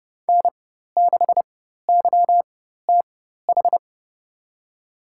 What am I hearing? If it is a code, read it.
Morse code, heard as a keyed tone, about 30 wpm, spaced as for 14 wpm, 700 Hz. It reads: N6YTH
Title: USA Call Signs - 4x Character Spacing - 30wpm